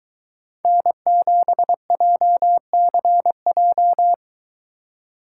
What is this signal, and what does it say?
Morse code, heard as a keyed tone, about 21 wpm, 700 Hz. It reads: N7JCJ
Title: Speed Racing - USA 1x3 Call Signs 17wpm